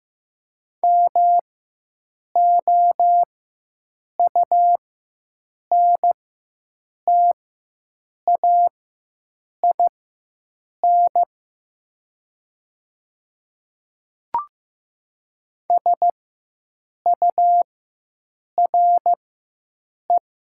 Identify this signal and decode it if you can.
Morse code, heard as a keyed tone, about 15 wpm, spaced as for 7 wpm, 700 Hz. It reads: MOUNTAIN SURE
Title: 4x Character Spacing - Top 300 Words - 15wpm